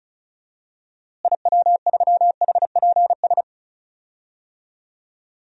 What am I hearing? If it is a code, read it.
Morse code, heard as a keyed tone, about 36 wpm, 700 Hz. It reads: IW3HPS
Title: DXCC-100 Call Signs - 6 Characters - Repeated 2x - 35wpm